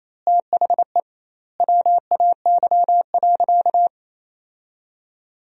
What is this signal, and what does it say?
Morse code, heard as a keyed tone, about 28 wpm, 700 Hz. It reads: THE WAY.
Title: 2x Word-Spacing - Sentences from Top 400 Words - 28wpm